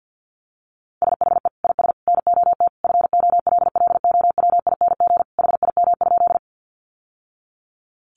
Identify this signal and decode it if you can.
Morse code, heard as a keyed tone, about 50 wpm, 700 Hz. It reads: SHE IS NOT FOLLOWING HIM?
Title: Sentences from Top 700 Words 50wpm